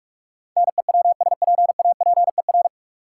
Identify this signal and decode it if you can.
Morse code, heard as a keyed tone, about 44 wpm, 700 Hz. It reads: NEWSPAPER
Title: Sentences from Top 1100 Words - New Word Drill Alt - 45wpm